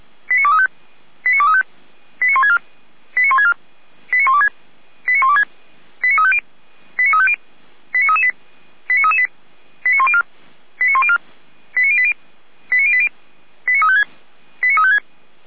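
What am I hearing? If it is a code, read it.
Five-tone selective call (ZVEI), heard as quick runs of five tones, tones of about 70 ms. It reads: 89235 89235 89164 89164 89126 89126 89350 89350 89338 89338 89104 89104 89080 89080 89356 89356
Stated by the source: ZVEI five-tone selective call received by SDR.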